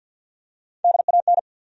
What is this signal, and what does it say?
Morse code, heard as a keyed tone, about 53 wpm, 700 Hz. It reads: DAN
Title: USA Common Names - Max 4 Letters - 50wpm